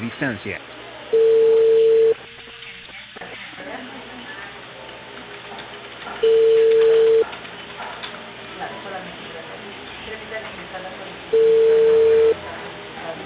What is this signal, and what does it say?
Ringback tone (China, 450 Hz), with speech.